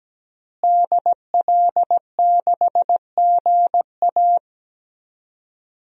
Morse code, heard as a keyed tone, about 17 wpm, 700 Hz. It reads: DL6GA
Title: DXCC-100 Call Signs - 5 Characters - Repeated 2x - 17wpm